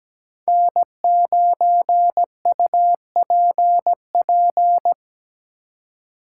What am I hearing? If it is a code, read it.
Morse code, heard as a keyed tone, about 20 wpm, 700 Hz. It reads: N9UPP
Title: Speed Racing - USA Call Signs 17wpm